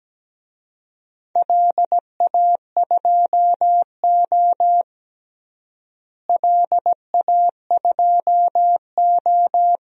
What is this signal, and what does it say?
Morse code, heard as a keyed tone, about 17 wpm, 700 Hz. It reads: LA2O LA2O
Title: DXCC-100 Call Signs - 4 Characters - Repeated 3x - 17wpm